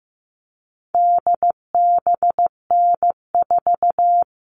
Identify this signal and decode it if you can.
Morse code, heard as a keyed tone, about 15 wpm, 700 Hz. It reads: DBN4
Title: All Cut Numbers - 4-Digit 15wpm